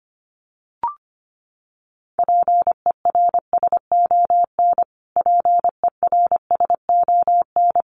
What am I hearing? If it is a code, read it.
Morse code, heard as a keyed tone, about 25 wpm, 700 Hz. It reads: PERSON PERSON
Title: Top 500 Words Repeated 25wpm